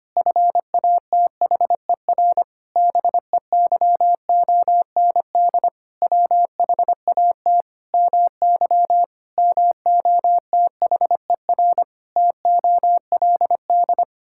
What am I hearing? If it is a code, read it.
Morse code, heard as a keyed tone, about 25 wpm, 700 Hz. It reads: FATHER BEYOND WHAT MY MOTHER TOLD
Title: Sentences from Top 1000 Words - Review Part 6 - 25wpm